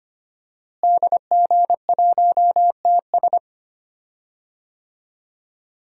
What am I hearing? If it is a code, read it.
Morse code, heard as a keyed tone, about 25 wpm, 700 Hz. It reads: DG1TS
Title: DXCC-100 Call Signs - 5 Characters - Repeated 2x - 25wpm